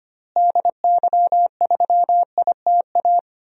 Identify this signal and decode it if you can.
Morse code, heard as a keyed tone, about 25 wpm, 700 Hz. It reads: DY3ITA
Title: DXCC-100 Call Signs - 6 Characters - Repeated 3x - 25wpm